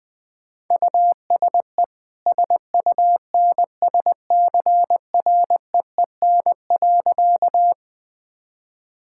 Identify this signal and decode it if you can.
Morse code, heard as a keyed tone, about 20 wpm, 700 Hz. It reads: USE SUNSCREEN.
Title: Easy Sentences with Hints - 20wpm